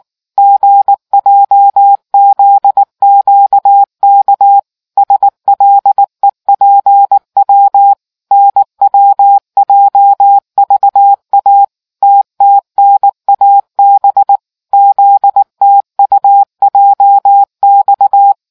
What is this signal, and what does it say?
Morse code, heard as a keyed tone, about 19 wpm, 800 Hz. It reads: GJZQK SLEPW NWJVA TTNAB ZTUJX